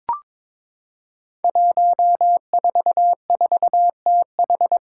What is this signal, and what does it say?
Morse code, heard as a keyed tone, about 23 wpm, 700 Hz. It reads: 144TH
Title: Five Challenging Letter-Number Combinations 22wpm